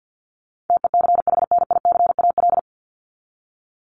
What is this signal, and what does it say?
Morse code, heard as a keyed tone, about 50 wpm, 700 Hz. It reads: TECHNICAL
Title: Top 1001-1250 Words 50wpm